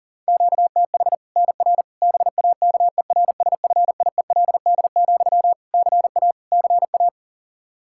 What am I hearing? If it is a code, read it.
Morse code, heard as a keyed tone, about 40 wpm, 700 Hz. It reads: QTH NR BAKERSFIELD, CA CA
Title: Practice QSO - Exchange 1 - Beginning - 40wpm